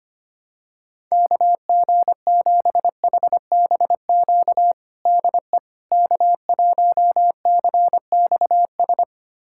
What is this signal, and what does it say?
Morse code, heard as a keyed tone, about 25 wpm, 700 Hz. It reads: KG7HBQ DE K1CXS
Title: USA Callsign DE Callsign 25wpm